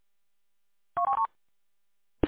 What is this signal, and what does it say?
There is no code.